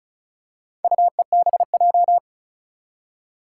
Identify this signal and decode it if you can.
Morse code, heard as a keyed tone, about 36 wpm, 700 Hz. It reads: UEBJ
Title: Four Challenging Letter-Number Combinations 35wpm